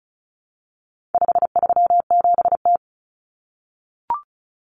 Morse code, heard as a keyed tone, about 34 wpm, 700 Hz. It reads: 537T
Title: Common Cut Numbers - 4-Digit 35wpm